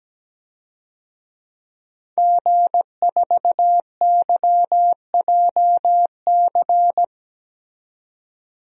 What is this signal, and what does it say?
Morse code, heard as a keyed tone, about 17 wpm, 700 Hz. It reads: G4YJC
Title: DXCC-100 Call Signs - 5 Characters - 17wpm